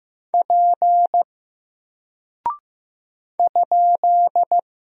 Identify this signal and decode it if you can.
Morse code, heard as a keyed tone, about 15 wpm, 700 Hz. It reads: P ?
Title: Single Character (Letter:Number:Punctuation:Prosign) Alt 15wpm